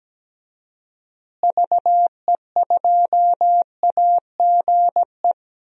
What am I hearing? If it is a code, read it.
Morse code, heard as a keyed tone, about 17 wpm, 700 Hz. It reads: VE2AGE
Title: DXCC-100 Call Signs - 6 Characters - Repeated 3x - 17wpm